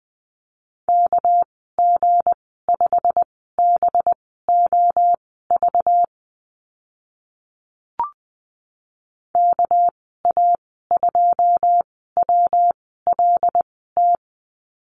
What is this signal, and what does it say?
Morse code, heard as a keyed tone, about 20 wpm, spaced as for 14 wpm, 700 Hz. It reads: KG5BOV KA2WLT
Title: 231 Lesson 37g - Letter x - Call Signs 20wpm